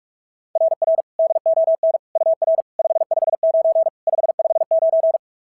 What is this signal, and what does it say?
Morse code, heard as a keyed tone, about 38 wpm, 660 Hz. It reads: RR DON UR 559 559
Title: SOTA QSO Practice 45wpm